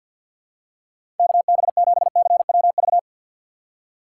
Morse code, heard as a keyed tone, about 52 wpm, 700 Hz. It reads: KB7CWV
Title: USA Call Signs - 6 Characters - 6x Character Spacing - 50wpm